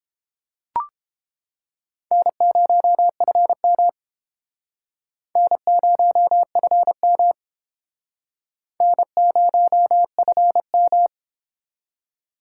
Morse code, heard as a keyed tone, about 30 wpm, 700 Hz. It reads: N0FM N0FM N0FM